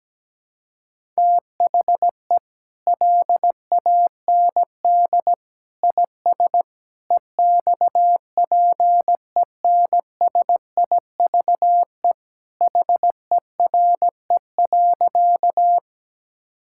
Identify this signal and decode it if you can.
Morse code, heard as a keyed tone, about 17 wpm, 700 Hz. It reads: THE LAND IS EXPENSIVE HERE.
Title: Sentences from Top 1000 Words - Review Part 6 - 17wpm